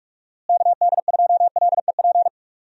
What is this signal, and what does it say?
Morse code, heard as a keyed tone, about 41 wpm, 700 Hz. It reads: KD2LEP
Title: USA Call Signs - 6 Characters - Repeated 3x - 45wpm